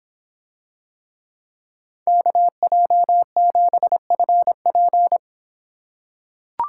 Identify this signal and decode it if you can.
Morse code, heard as a keyed tone, about 24 wpm, 700 Hz. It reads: KJ7FP